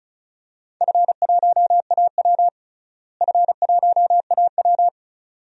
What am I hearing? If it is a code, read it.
Morse code, heard as a keyed tone, about 36 wpm, 700 Hz. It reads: F1AW F1AW
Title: DXCC-100 Call Signs - 4 Characters - Repeated 2x - 35wpm